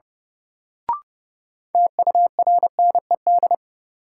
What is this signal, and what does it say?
Morse code, heard as a keyed tone, about 31 wpm, 700 Hz. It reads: TURNED
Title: Sets of 4 Words - New Word Drill - 30wpm